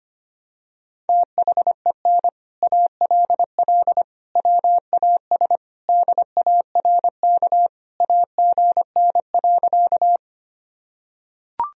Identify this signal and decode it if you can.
Morse code, heard as a keyed tone, about 25 wpm, 700 Hz. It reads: THEN ALL WAS DARK AGN.
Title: Sentences from Top 500 Words and Abbreviations 25wpm